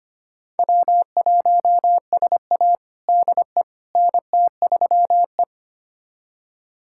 Morse code, heard as a keyed tone, about 25 wpm, 700 Hz. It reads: W1SA DE NT3E
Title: USA Callsign DE Callsign - 4 Characters - 25wpm